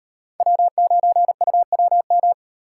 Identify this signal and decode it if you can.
Morse code, heard as a keyed tone, about 36 wpm, 700 Hz. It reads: W9UWM